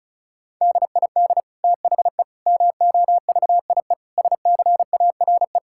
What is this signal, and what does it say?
Morse code, heard as a keyed tone, about 36 wpm, 700 Hz. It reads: DID THE MOVIE SCARE
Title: Sentences from Top 1000 Words - Review Part 5 - 35wpm